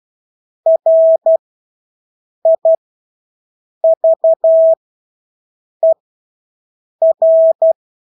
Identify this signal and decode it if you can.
Morse code, heard as a keyed tone, about 12 wpm, spaced as for 6 wpm, 650 Hz. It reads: RIVER
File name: Engelske ord 12-6 wpm